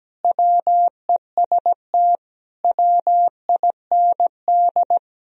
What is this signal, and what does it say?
Morse code, heard as a keyed tone, about 17 wpm, 700 Hz. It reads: WEST WIND
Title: Sentences from Top 1000 Words - Review Part 2 - 17wpm